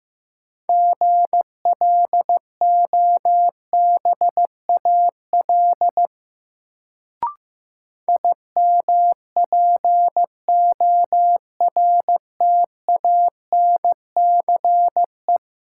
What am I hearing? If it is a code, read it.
Morse code, heard as a keyed tone, about 15 wpm, 700 Hz. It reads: GLOBAL IMPORTANCE
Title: Sentences from Top 1200 Words - New Word Drill Alt - 15wpm